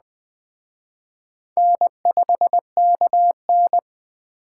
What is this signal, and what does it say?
Morse code, heard as a keyed tone, about 20 wpm, 700 Hz. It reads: N5KN
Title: Speed Racing - USA 1x2 Call Signs 17wpm